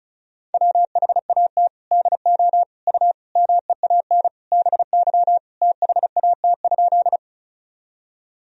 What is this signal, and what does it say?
Morse code, heard as a keyed tone, about 34 wpm, 700 Hz. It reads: WHAT DO U MEAN BY THAT?
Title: Sentences from Top 200 Words and Abbreviations 35wpm